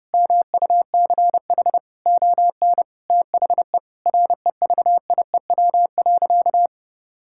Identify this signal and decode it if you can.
Morse code, heard as a keyed tone, about 29 wpm, 700 Hz. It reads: MUCH ON THE REVIEW.
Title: Sentences from Top 1000 Words - Review Part 6 - 30wpm